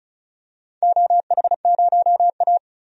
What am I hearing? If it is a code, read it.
Morse code, heard as a keyed tone, about 36 wpm, 700 Hz. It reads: OH0A
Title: DXCC-100 Call Signs - 4 Characters - 35wpm